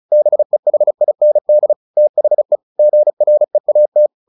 Morse code, heard as a keyed tone, about 34 wpm, 600 Hz. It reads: BEHIND THE GREAT
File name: the old man and the sea 2 35wpm